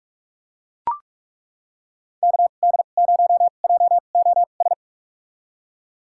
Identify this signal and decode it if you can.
Morse code, heard as a keyed tone, about 41 wpm, spaced as for 31 wpm, 690 Hz. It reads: KD0JOS
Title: USA Call Signs - 6 Characters - 2x Character Spacing - 45wpm